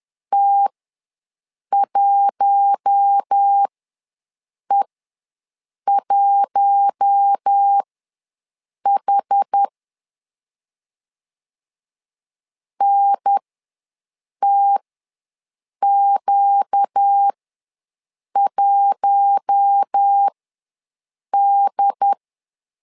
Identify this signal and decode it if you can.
Morse code, heard as a keyed tone, about 10 wpm, spaced as for 6 wpm, 800 Hz. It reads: T1E1H NTQ1D
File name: LEZIONI / 4 Q1 ws 10wpm 2538